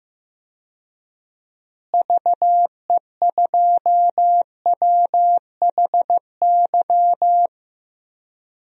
Morse code, heard as a keyed tone, about 15 wpm, 700 Hz. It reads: VE2WHY
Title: DXCC-100 Call Signs - 6 Characters - Repeated 3x - 15wpm